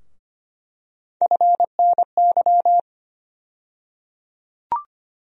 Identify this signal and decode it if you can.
Morse code, heard as a keyed tone, about 25 wpm, 700 Hz. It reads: FNY
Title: USA Call Sign Postfixes - 3 Letters 25wpm